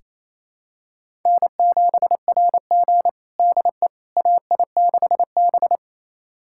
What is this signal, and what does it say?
Morse code, heard as a keyed tone, about 28 wpm, 700 Hz. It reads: N7RG DE AI6B
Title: USA Callsign DE Callsign - 4 Characters - 28wpm